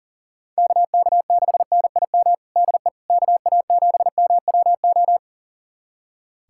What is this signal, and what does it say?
Morse code, heard as a keyed tone, about 40 wpm, 700 Hz. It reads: KK6NIM DE KA7MWO
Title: USA Callsign DE Callsign - 6 Characters - 40wpm